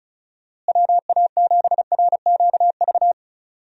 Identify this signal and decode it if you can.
Morse code, heard as a keyed tone, about 36 wpm, 700 Hz. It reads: WA7RQV
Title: USA Call Signs - 6 Characters - Repeated 3x - 35wpm